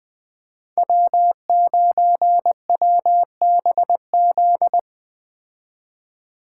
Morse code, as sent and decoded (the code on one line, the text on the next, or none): .-- ----. .-- -... --..
W9WBZ